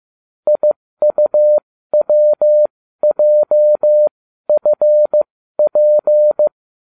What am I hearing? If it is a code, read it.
Morse code, heard as a keyed tone, about 15 wpm, spaced as for 12 wpm, 600 Hz. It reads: IUWJFP
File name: cw / ubacw20 3